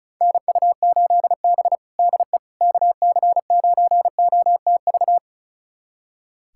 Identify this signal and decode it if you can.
Morse code, heard as a keyed tone, about 36 wpm, 700 Hz. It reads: NU8B DE KC9OTV
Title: USA Callsign DE Callsign 35wpm